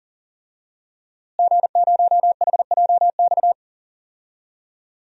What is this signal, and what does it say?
Morse code, heard as a keyed tone, about 39 wpm, 700 Hz. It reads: G0HJX
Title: DXCC-100 Call Signs - 5 Characters - Repeated 2x - 40wpm